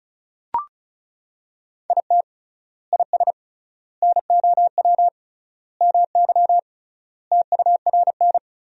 Morse code, heard as a keyed tone, about 34 wpm, 700 Hz. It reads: IT IS NOW MY TURN
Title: 3x Word-Spacing - Sentences from Top 200 Words - 35wpm